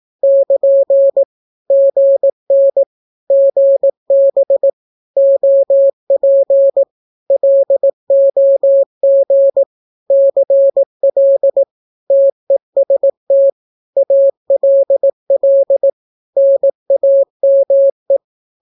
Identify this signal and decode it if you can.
Morse code, heard as a keyed tone, about 18 wpm, 550 Hz. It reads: <KN> GN GB OP LOG CL TEST ALL NAME
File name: naukapodstaw / skroty / skroty